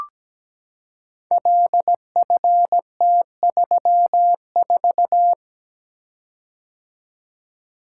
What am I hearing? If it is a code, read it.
Morse code, heard as a keyed tone, about 17 wpm, 700 Hz. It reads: LFT34
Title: Code Groups - 5 Letter:Numbers - 17wpm